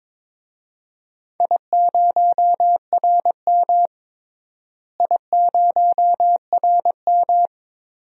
Morse code, heard as a keyed tone, about 22 wpm, 700 Hz. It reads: I0RM I0RM
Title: DXCC-100 Call Signs - 4 Characters - Repeated 2x - 22wpm